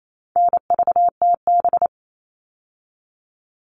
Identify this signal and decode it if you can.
Morse code, heard as a keyed tone, about 28 wpm, 700 Hz. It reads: NVTB
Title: All Cut Numbers - 4-Digit 28wpm